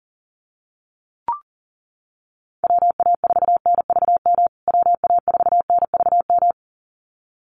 Morse code, heard as a keyed tone, about 40 wpm, 700 Hz. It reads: WA4NVM WA4NVM
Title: USA Call Signs Repeated 2x 40wpm